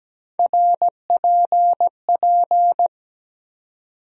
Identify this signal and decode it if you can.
Morse code, heard as a keyed tone, about 17 wpm, 700 Hz. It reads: RPP
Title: USA Call Sign Postfixes - 3 Letters 17wpm